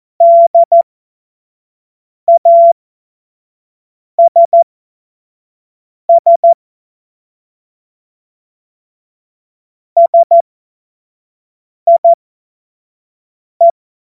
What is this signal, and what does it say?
Morse code, heard as a keyed tone, about 14 wpm, spaced as for 5 wpm, 680 Hz. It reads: DASS SIE
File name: Morsekurs / Witze / Witz 23. Mai 2023 25 F70